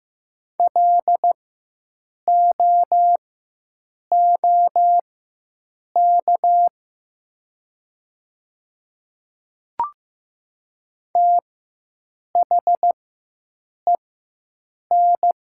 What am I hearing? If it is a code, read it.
Morse code, heard as a keyed tone, about 15 wpm, spaced as for 7 wpm, 700 Hz. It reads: LOOK THEN
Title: 4x Character Spacing - Top 100 Words - 15wpm